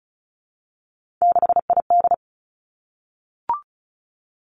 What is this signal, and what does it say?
Morse code, heard as a keyed tone, about 36 wpm, 700 Hz. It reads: 6ID
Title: Three Challenging Letter-Number Combinations 35wpm